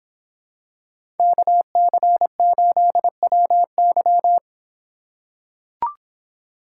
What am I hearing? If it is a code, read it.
Morse code, heard as a keyed tone, about 26 wpm, 700 Hz. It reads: KC8WY